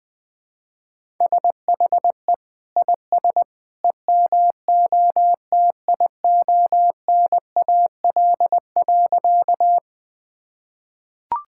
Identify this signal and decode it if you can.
Morse code, heard as a keyed tone, about 20 wpm, 700 Hz. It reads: SHE IS EMOTIONAL.
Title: Sentences from Top 1000 Words - Review Part 5 - 20wpm